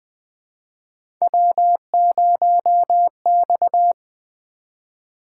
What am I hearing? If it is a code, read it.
Morse code, heard as a keyed tone, about 20 wpm, 700 Hz. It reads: W0X
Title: Three Letter-Number Combinations - 20wpm